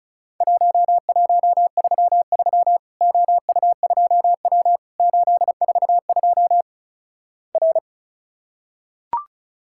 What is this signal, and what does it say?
Morse code, heard as a keyed tone, about 36 wpm, 700 Hz. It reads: 1133 OU2W 842 R
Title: Worked All Europe DX Contest - QTC Exchange - 35wpm